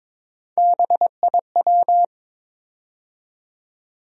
Morse code, heard as a keyed tone, about 22 wpm, 700 Hz. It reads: BIW
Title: USA Call Sign Postfixes - 3 Letters 22wpm